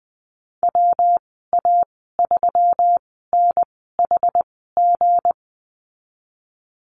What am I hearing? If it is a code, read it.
Morse code, heard as a keyed tone, about 20 wpm, spaced as for 15 wpm, 700 Hz. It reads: WA3NHG
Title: 167 Lesson 27g - Number 7 - Call Signs 20wpm